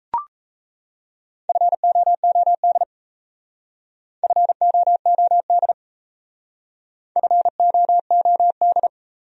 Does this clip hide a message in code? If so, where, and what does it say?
Yes: Morse code from 1.5 to 8.9 s, heard as a keyed tone, about 38 wpm, 700 Hz. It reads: FOOD FOOD FOOD